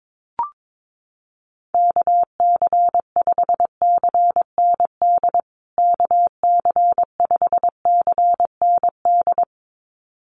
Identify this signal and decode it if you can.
Morse code, heard as a keyed tone, about 22 wpm, 700 Hz. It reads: KC5CND KC5CND
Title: USA Call Signs Repeated 2x 22wpm